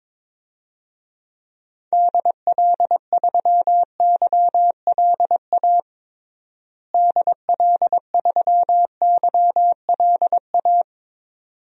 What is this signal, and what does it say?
Morse code, heard as a keyed tone, about 22 wpm, 700 Hz. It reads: DL3YLA DL3YLA
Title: DXCC-100 Call Signs - 6 Characters - Repeated 2x - 22wpm